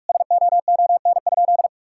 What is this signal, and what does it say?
Morse code, heard as a keyed tone, about 46 wpm, 700 Hz. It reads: SOON?
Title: Sentences from Top 1000 Words - Review Part 6 - 45wpm